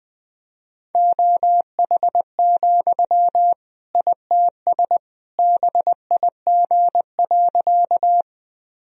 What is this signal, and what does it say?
Morse code, heard as a keyed tone, about 20 wpm, 700 Hz. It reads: OH, ITS BIG.
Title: Sentences from Top 1000 Words - Review Part 1 - 20wpm